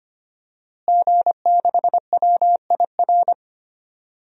Morse code, heard as a keyed tone, about 25 wpm, 700 Hz. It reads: G6WIR
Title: DXCC-100 Call Signs - 5 Characters - Repeated 3x - 25wpm